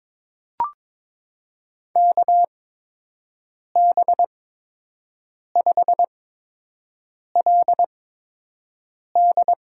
Morse code, heard as a keyed tone, about 22 wpm, spaced as for 6 wpm, 700 Hz. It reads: KB5LD
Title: USA Call Signs - 8x Character Spacing - 22wpm